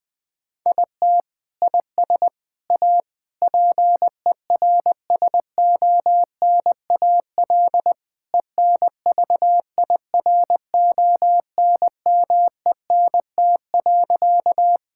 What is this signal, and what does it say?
Morse code, heard as a keyed tone, about 20 wpm, 700 Hz. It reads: IT IS A PERSONAL ENVIRONMENT.
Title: Sentences from Top 1000 Words - Review Part 5 - 20wpm